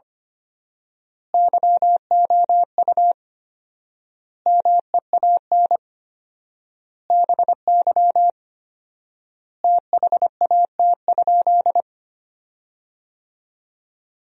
Morse code, heard as a keyed tone, about 25 wpm, 700 Hz. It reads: YOU MEAN BY THAT?
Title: 4x Word-Spacing - Sentences from Top 200 Words - 25wpm